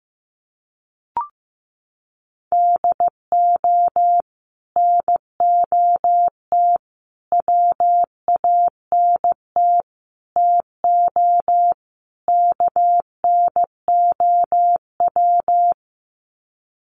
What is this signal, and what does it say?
Morse code, heard as a keyed tone, about 15 wpm, 700 Hz. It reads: DO NOT WANT TO KNOW
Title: Sets of 5 Words Encore 15wpm